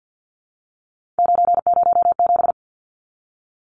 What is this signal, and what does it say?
Morse code, heard as a keyed tone, about 50 wpm, 700 Hz. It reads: 907
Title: Numbers - 3-Digit 50wpm